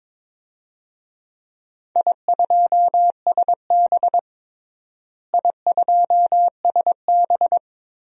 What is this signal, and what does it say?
Morse code, heard as a keyed tone, about 22 wpm, 700 Hz. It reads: I2SB I2SB
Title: DXCC-100 Call Signs - 4 Characters - Repeated 3x - 22wpm